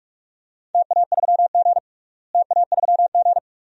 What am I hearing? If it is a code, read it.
Morse code, heard as a keyed tone, about 44 wpm, 700 Hz. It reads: TA3G TA3G
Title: DXCC-100 Call Signs - 4 Characters - Repeated 2x - 45wpm